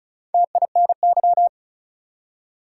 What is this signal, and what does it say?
Morse code, heard as a keyed tone, about 36 wpm, 700 Hz. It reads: TINY
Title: Four Letter Words - 35wpm